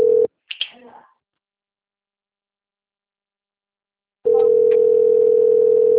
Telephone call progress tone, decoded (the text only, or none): ringback tone